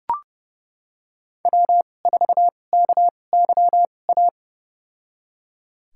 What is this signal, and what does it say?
Morse code, heard as a keyed tone, about 32 wpm, spaced as for 22 wpm, 700 Hz. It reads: W4KYA
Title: USA Call Signs - 2x Character Spacing - 30wpm